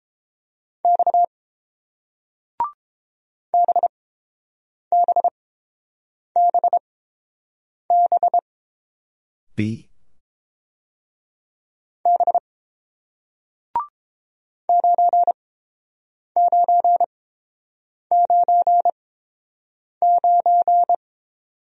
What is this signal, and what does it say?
Morse code, heard as a keyed tone, about 30 wpm, spaced as for 6 wpm, 700 Hz. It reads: X BBBBB 9999